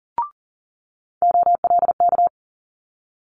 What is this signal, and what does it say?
Morse code, heard as a keyed tone, about 43 wpm, 700 Hz. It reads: OLK
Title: Multi-letter Phonemes Quick Fire 40wpm